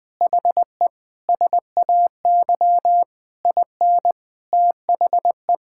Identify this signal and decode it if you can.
Morse code, heard as a keyed tone, about 21 wpm, 700 Hz. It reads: HE SAY IN THE
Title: Sentences from Top 1000 Words - Review Part 6 - 20wpm